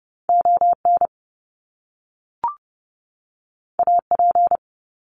Morse code, heard as a keyed tone, about 30 wpm, 700 Hz. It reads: ON AP
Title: Top Two Letter Combinations in Words 30wpm